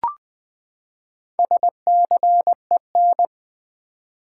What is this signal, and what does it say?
Morse code, heard as a keyed tone, about 20 wpm, 700 Hz. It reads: SCEN
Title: Build A Word - 4 Letters At A Time - 8-Letter Words - 20wpm